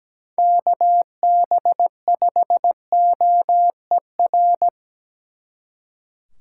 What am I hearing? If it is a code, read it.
Morse code, heard as a keyed tone, about 20 wpm, 700 Hz. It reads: KB5OER
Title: Speed Racing - USA 2x3 Call Signs 17wpm